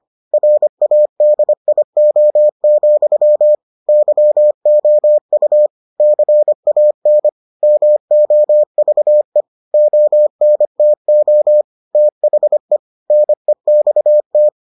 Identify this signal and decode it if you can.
Morse code, heard as a keyed tone, about 25 wpm, 600 Hz. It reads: RADIO, YOU CAN MOVE ONTO THE NEXT